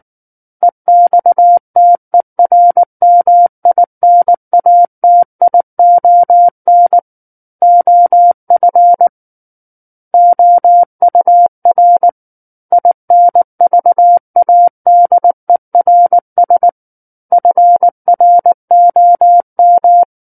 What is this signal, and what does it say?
Morse code, heard as a keyed tone, about 19 wpm, 700 Hz. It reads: EXTERMINATION OF OUR INVADERS FROM
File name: Book One / 10-Chapter 10 - 19 WPM